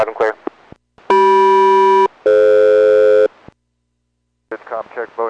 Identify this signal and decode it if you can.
Aircraft SELCAL, heard as two pairs of tones, with speech. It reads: CM-DF